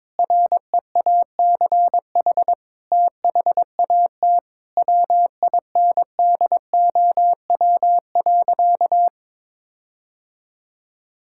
Morse code, heard as a keyed tone, about 22 wpm, 700 Hz. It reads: REACH THAT WINDOW.
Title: Sentences from Top 1600 Words 22wpm